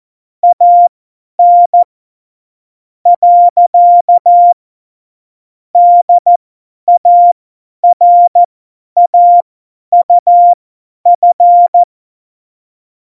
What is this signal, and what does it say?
Morse code, heard as a keyed tone, about 14 wpm, spaced as for 9 wpm, 700 Hz. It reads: AN. DARAUF
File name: Morsekurs / Witze / Witz 12. Dezember 2017, 50 F70